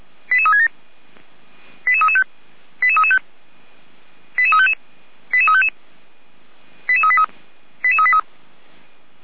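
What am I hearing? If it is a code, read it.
Five-tone selective call (ZVEI), heard as quick runs of five tones, tones of about 80 ms. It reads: 88367 88335 88335 88355 88355 88383 88383
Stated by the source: ZVEI five-tone selective call 88365 received on an SDR.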